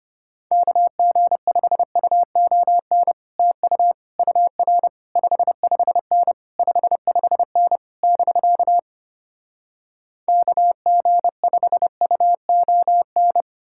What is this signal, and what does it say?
Morse code, heard as a keyed tone, about 33 wpm, 700 Hz. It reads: KG5UON TU UR 55N 55N <BK> KG5UON